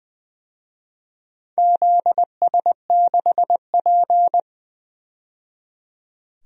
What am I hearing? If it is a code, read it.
Morse code, heard as a keyed tone, about 21 wpm, 700 Hz. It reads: ZS6P
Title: DXCC-100 Call Signs - 4 Characters - 20wpm